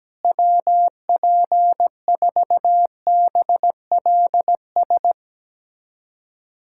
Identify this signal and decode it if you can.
Morse code, heard as a keyed tone, about 17 wpm, 700 Hz. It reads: WP4BLS
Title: USA Callsign DE Callsign - 6 Characters - 17wpm